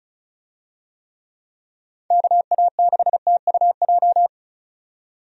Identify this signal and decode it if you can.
Morse code, heard as a keyed tone, about 36 wpm, 700 Hz. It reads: KA6TUJ
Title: USA Call Signs - 6 Characters - 35wpm